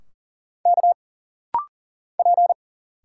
Morse code, heard as a keyed tone, about 40 wpm, 700 Hz. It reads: K P
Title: Single Letters - Rapid-Fire 40wpm